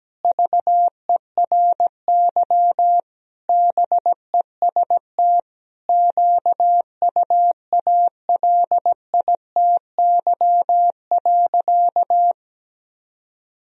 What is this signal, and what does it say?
Morse code, heard as a keyed tone, about 17 wpm, 700 Hz. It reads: VERY BEST QUALITY.
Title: Sentences from Top 1000 Words - Review Part 4 - 17wpm